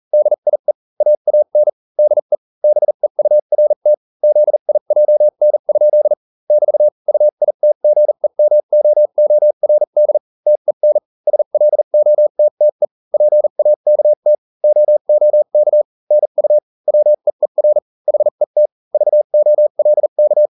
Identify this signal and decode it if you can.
Morse code, heard as a keyed tone, about 40 wpm, spaced as for 35 wpm, 600 Hz. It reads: DIE AAN DE BEURT ZIJN? <BT> UITGEMOORD TEN SLOTTE PAKT OOK NU WEER HET VOLK